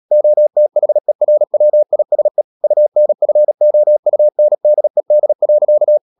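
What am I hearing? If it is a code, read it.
Morse code, heard as a keyed tone, about 37 wpm, 600 Hz. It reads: OTHERWISE UNFOUNDED.
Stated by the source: Morse code practice at 35–39 wpm sending English text of Ken Follett's novel Never, from CW med Gnister.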